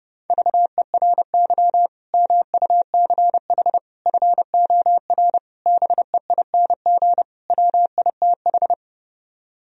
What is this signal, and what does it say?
Morse code, heard as a keyed tone, about 33 wpm, 700 Hz. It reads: VERY MUCH FOR BEING WITH